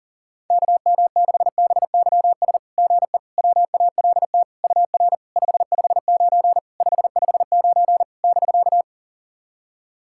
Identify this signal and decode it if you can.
Morse code, heard as a keyed tone, about 39 wpm, 690 Hz. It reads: KM6BYS GE WALT UR 559 559 <BK>
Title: SOTA QSO Practice 40wpm